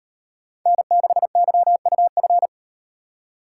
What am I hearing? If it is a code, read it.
Morse code, heard as a keyed tone, about 36 wpm, 700 Hz. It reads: N6YUF